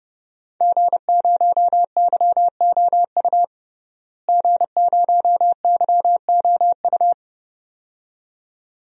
Morse code, heard as a keyed tone, about 30 wpm, 700 Hz. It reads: G0YOU G0YOU
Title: DXCC-100 Call Signs - 5 Characters - Repeated 2x - 30wpm